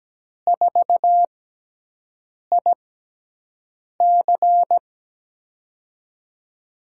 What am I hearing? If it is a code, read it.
Morse code, heard as a keyed tone, about 17 wpm, spaced as for 6 wpm, 700 Hz. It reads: 4IC
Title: USA Call Signs - 6x Character Spacing - 17wpm